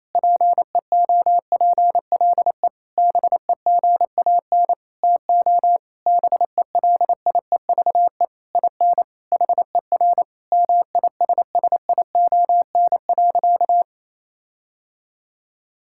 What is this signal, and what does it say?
Morse code, heard as a keyed tone, about 28 wpm, 700 Hz. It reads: PEOPLE BEGAN TO BELIEVE IN HER MISSION.
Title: Sentences from Top 1600 Words 28wpm